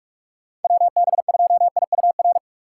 Morse code, heard as a keyed tone, about 46 wpm, 700 Hz. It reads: WB2IUR
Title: USA Call Signs - 6 Characters - 3x Character Spacing - 45wpm